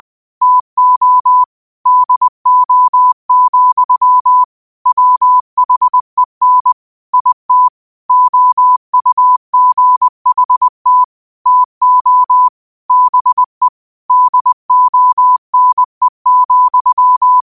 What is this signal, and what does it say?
Morse code, heard as a keyed tone, about 20 wpm, 1000 Hz. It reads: TO DO, WHEN IT OUGHT TO BE DONE,
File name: qotdfiles / 20wpm